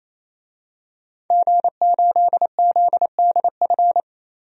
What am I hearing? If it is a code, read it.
Morse code, heard as a keyed tone, about 27 wpm, 700 Hz. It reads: G8ZDF
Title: DXCC-100 Call Signs - 5 Characters - 28wpm